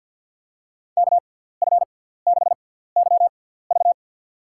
Morse code, heard as a keyed tone, about 52 wpm, spaced as for 17 wpm, 700 Hz. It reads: KF6YV
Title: USA Call Signs - 5 Characters - 6x Character Spacing - 50wpm